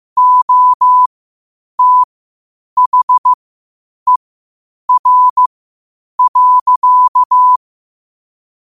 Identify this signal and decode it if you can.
Morse code, heard as a keyed tone, about 15 wpm, spaced as for 8 wpm, 1000 Hz. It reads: OTHER.
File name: qotdfiles / 8wpm